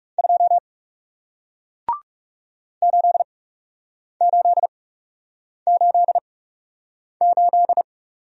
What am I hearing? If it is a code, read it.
Morse code, heard as a keyed tone, about 40 wpm, spaced as for 8 wpm, 700 Hz. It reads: 2 8888